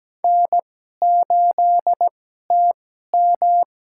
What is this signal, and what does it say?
Morse code, heard as a keyed tone, about 17 wpm, spaced as for 12 wpm, 700 Hz. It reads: N8TM
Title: USA Call Signs - 4 Characters - 2x Character Spacing - 17wpm